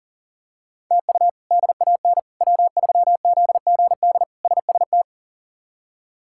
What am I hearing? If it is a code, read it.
Morse code, heard as a keyed tone, about 40 wpm, 700 Hz. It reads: TU DAN W3ZGD SST
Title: K1USN SST QSO Practice - 40wpm